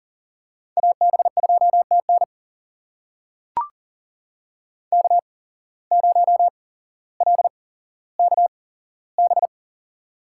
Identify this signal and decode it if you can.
Morse code, heard as a keyed tone, about 40 wpm, spaced as for 17 wpm, 700 Hz. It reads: AB2TN K0LKB
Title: USA Call Signs - 8x Character Spacing - 40wpm